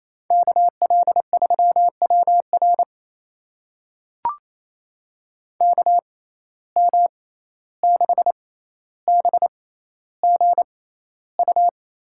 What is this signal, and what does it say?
Morse code, heard as a keyed tone, about 28 wpm, spaced as for 14 wpm, 700 Hz. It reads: KL3WR KM6BGU
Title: USA Call Signs - 6x Character Spacing - 28wpm